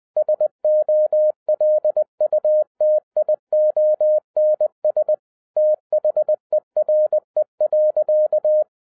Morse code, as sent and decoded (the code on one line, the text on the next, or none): ... --- .-.. ..- - .. --- -. ... / - .... .-. .-.-.-
SOLUTIONS THERE.